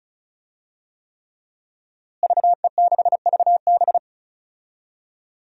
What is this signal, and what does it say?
Morse code, heard as a keyed tone, about 36 wpm, 700 Hz. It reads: VE6VB
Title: DXCC-100 Call Signs - 5 Characters - Repeated 3x - 35wpm